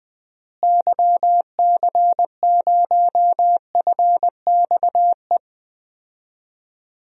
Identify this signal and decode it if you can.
Morse code, heard as a keyed tone, about 20 wpm, 700 Hz. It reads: YC0FXE
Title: DXCC-100 Call Signs - 6 Characters - Repeated 2x - 20wpm